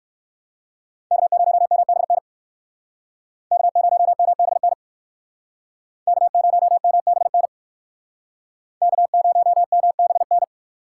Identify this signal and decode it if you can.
Morse code, heard as a keyed tone, about 59 wpm, 700 Hz. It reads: K0MBN K0MBN K0MBN K0MBN